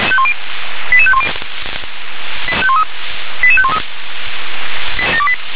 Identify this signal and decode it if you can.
Five-tone selective call (ZVEI), heard as quick runs of five tones, tones of about 70 ms. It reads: 88510 88513